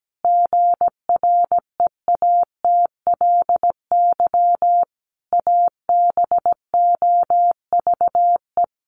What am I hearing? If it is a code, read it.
Morse code, heard as a keyed tone, about 17 wpm, 700 Hz. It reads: GREATLY ABOVE
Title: Sentences from Top 1400 Words 17wpm